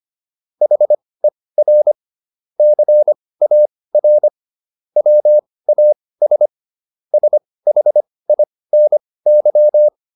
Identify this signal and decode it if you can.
Morse code, heard as a keyed tone, about 25 wpm, spaced as for 18 wpm, 600 Hz. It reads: HER CAR WAS SHINY